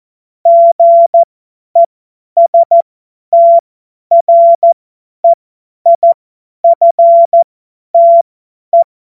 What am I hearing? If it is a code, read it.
Morse code, heard as a keyed tone, about 14 wpm, spaced as for 10 wpm, 680 Hz. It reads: GESTREIFTE
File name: Morsekurs / Witze / Witz 18. Juli 2023 50 F70